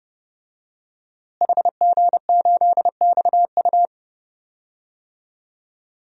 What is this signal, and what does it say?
Morse code, heard as a keyed tone, about 31 wpm, 700 Hz. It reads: HG8XU
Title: DXCC-100 Call Signs - 5 Characters - Repeated 3x - 30wpm